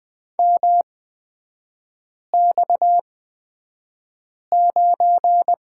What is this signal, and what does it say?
Morse code, heard as a keyed tone, about 20 wpm, spaced as for 5 wpm, 690 Hz. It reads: MX9
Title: 229 Lesson 37e - Letter x - Three Characters 20wpm